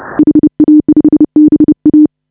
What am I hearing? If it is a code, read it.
Morse code, heard as a keyed tone, about 30 wpm, 310 Hz. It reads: HA5BA